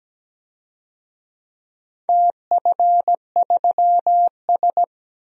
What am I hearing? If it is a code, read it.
Morse code, heard as a keyed tone, about 17 wpm, 700 Hz. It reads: TF3S
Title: DXCC-100 Call Signs - 4 Characters - Repeated 2x - 17wpm